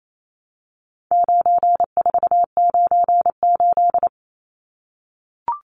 Morse code, heard as a keyed tone, about 28 wpm, 700 Hz. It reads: 9498
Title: Numbers - 4-Digit 28wpm